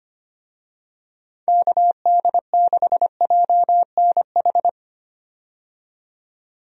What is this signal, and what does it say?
Morse code, heard as a keyed tone, about 25 wpm, 700 Hz. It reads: KD6JNH
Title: USA Call Signs - 6 Characters - 25wpm